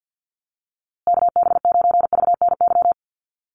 Morse code, heard as a keyed tone, about 50 wpm, 700 Hz. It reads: KB9VNY
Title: USA Call Signs Repeated 3x 50wpm